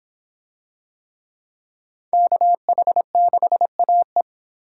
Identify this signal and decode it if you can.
Morse code, heard as a keyed tone, about 24 wpm, 700 Hz. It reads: KH6AE